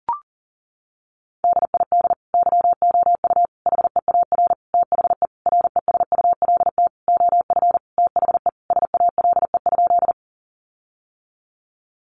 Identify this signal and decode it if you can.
Morse code, heard as a keyed tone, about 40 wpm, 700 Hz. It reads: DID YOU HEAR THE RESULT OF THE SALE?
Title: Sentences from Top 1400 Words 40wpm